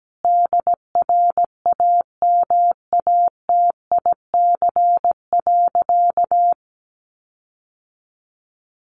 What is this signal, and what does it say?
Morse code, heard as a keyed tone, about 17 wpm, 700 Hz. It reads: DRAMATIC.
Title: Sentences from Top 1300 Words 17wpm